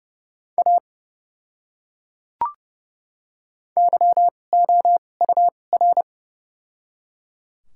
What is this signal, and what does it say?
Morse code, heard as a keyed tone, about 30 wpm, spaced as for 22 wpm, 700 Hz. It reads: A YOUR
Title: 2x Character Spacing - Top 100 Words - 30wpm